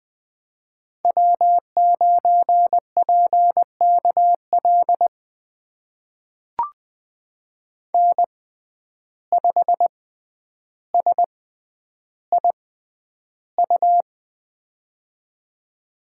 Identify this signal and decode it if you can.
Morse code, heard as a keyed tone, about 20 wpm, spaced as for 10 wpm, 700 Hz. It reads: W9PKL N5SIU
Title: USA Call Signs - 5 Characters - 6x Character Spacing - 20wpm